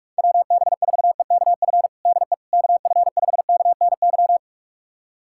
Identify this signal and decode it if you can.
Morse code, heard as a keyed tone, about 41 wpm, 700 Hz. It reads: WB4EKF DE KF5KNY
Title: USA Callsign DE Callsign - 6 Characters - 45wpm